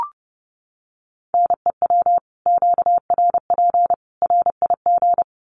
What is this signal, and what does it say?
Morse code, heard as a keyed tone, about 30 wpm, 700 Hz. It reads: NEW QRP RIG
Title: Common QSO Phrases 30wpm